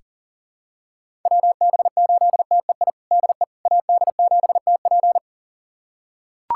Morse code, heard as a keyed tone, about 40 wpm, 700 Hz. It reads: WB8TEI DE AD7TP
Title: USA Callsign DE Callsign 40wpm